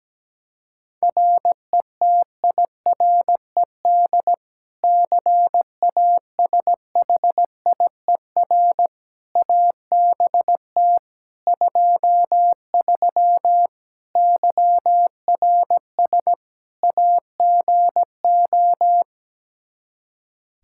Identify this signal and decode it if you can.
Morse code, heard as a keyed tone, about 17 wpm, 700 Hz. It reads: RETIRED CASHIER ABT 23 YRS AGO
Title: Practice QSO - Exchange 3 - More Info - 17wpm